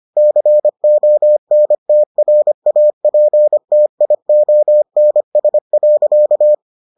Morse code, heard as a keyed tone, about 25 wpm, 600 Hz. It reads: CONTRAPTIONS.